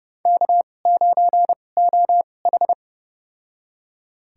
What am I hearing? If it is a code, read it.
Morse code, heard as a keyed tone, about 30 wpm, spaced as for 22 wpm, 700 Hz. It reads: K9OH
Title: USA Call Signs - 2x Character Spacing - 30wpm